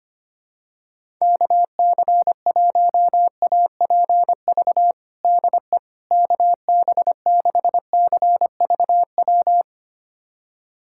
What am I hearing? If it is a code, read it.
Morse code, heard as a keyed tone, about 25 wpm, 700 Hz. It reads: KC1APV DE KB6CVW
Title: USA Callsign DE Callsign 25wpm